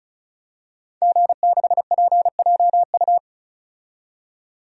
Morse code, heard as a keyed tone, about 34 wpm, 700 Hz. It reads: G6PJU
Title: DXCC-100 Call Signs - 5 Characters - 35wpm